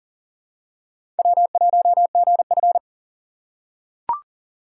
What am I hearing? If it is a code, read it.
Morse code, heard as a keyed tone, about 40 wpm, 700 Hz. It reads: W1GF
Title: USA Call Signs - 4 Characters - 3x Character Spacing - 40wpm